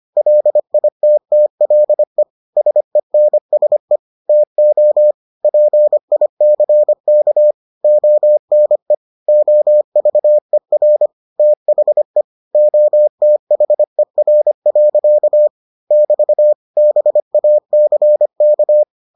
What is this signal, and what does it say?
Morse code, heard as a keyed tone, about 25 wpm, 600 Hz. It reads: LITTLE SENSE TO PICK ONE OVER THE OTHER. <BT> BACK